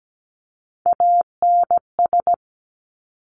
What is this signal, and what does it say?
Morse code, heard as a keyed tone, about 17 wpm, 700 Hz. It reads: ANS
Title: Top Three Letter Combinations in Words 17wpm